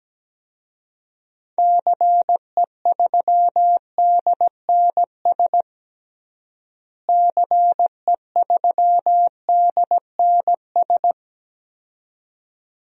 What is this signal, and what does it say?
Morse code, heard as a keyed tone, about 17 wpm, 700 Hz. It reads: CE3DNS CE3DNS
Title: DXCC-100 Call Signs - 6 Characters - Repeated 2x - 17wpm